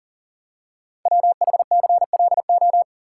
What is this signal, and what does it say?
Morse code, heard as a keyed tone, about 38 wpm, 690 Hz. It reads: WHCLO
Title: Code Groups - 5 Letter:Numbers - 40wpm